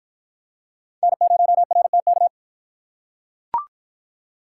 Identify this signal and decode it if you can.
Morse code, heard as a keyed tone, about 44 wpm, 700 Hz. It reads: N0RTK
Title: Speed Racing - USA 1x3 Call Signs 35wpm